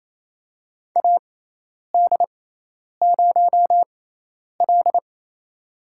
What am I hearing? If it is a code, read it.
Morse code, heard as a keyed tone, about 27 wpm, spaced as for 10 wpm, 700 Hz. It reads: AD0L
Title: USA Call Signs - 4 Characters - 6x Character Spacing - 28wpm